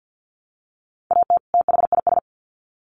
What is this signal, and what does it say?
Morse code, heard as a keyed tone, about 50 wpm, 700 Hz. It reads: AT THIS
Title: Sets of 2 Words 50wpm